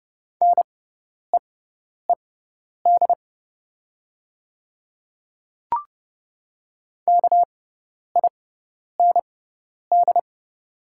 Morse code, heard as a keyed tone, about 30 wpm, spaced as for 10 wpm, 700 Hz. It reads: NEED KIND
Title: 6x Character Spacing - Top 200 Words - 30wpm